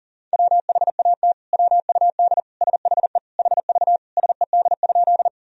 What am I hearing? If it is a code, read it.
Morse code, heard as a keyed tone, about 40 wpm, 700 Hz. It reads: WHAT WUD SHE HV SED?
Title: Sentences from Top 100 Words and Abbreviations 40wpm